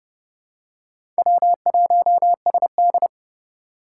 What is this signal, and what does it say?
Morse code, heard as a keyed tone, about 30 wpm, 700 Hz. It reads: W1SD
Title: USA Call Signs - 2x Character Spacing - 30wpm